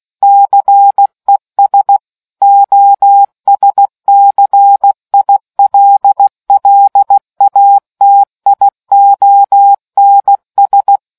Morse code, heard as a keyed tone, about 16 wpm, 800 Hz. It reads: CES OSCILLATIONS